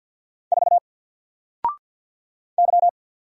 Morse code, heard as a keyed tone, about 48 wpm, 700 Hz. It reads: 4 Y
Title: Single Character (Letter:Number:Punctuation:Prosign) Alt 50wpm